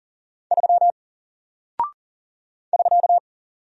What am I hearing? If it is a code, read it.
Morse code, heard as a keyed tone, about 40 wpm, 700 Hz. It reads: 3 <SK>
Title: Single Character (Letter:Number:Punctuation:Prosign) Alt 40wpm